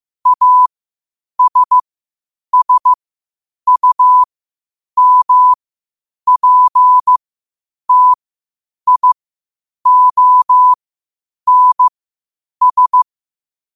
Morse code, heard as a keyed tone, about 15 wpm, spaced as for 8 wpm, 1000 Hz. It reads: ASSUMPTIONS
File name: qotdfiles / 8wpm